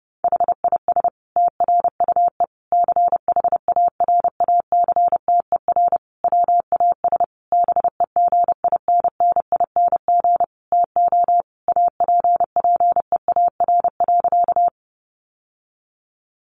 Morse code, heard as a keyed tone, about 30 wpm, 700 Hz. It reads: HIS TRUE CHARACTER WAS BEGINNING TO APPEAR.
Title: Sentences from Top 1200 Words 30wpm